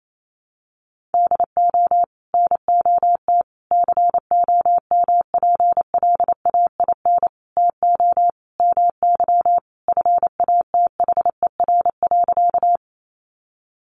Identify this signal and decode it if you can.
Morse code, heard as a keyed tone, about 28 wpm, 700 Hz. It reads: DO NOT COMPLAIN TO MY FATHER.
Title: Sentences from Top 1300 Words 28wpm